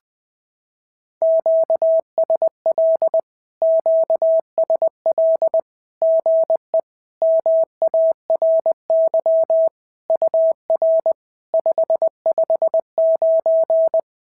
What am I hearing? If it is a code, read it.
Morse code, heard as a keyed tone, about 20 wpm, 650 Hz. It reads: QSL QSL GE MARY UR 559
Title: SOTA QSO Practice 20wpm